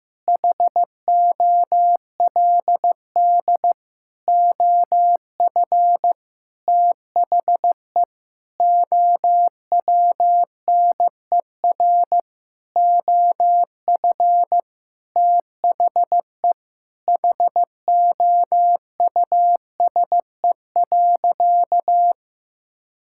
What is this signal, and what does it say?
Morse code, heard as a keyed tone, about 15 wpm, 700 Hz. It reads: HOLD OF THE OWNER OF THE HOUSE.
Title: Sentences from Top 1600 Words 15wpm